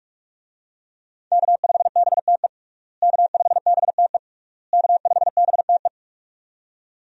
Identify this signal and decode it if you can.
Morse code, heard as a keyed tone, about 46 wpm, 700 Hz. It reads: K5BTE K5BTE K5BTE
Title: USA Call Signs - 5 Characters - Repeated 3x - 45wpm